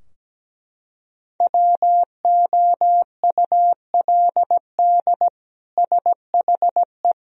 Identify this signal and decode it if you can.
Morse code, heard as a keyed tone, about 17 wpm, 700 Hz. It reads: WOULD SHE
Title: Sentences from Top 1000 Words - Review Part 3 - 17wpm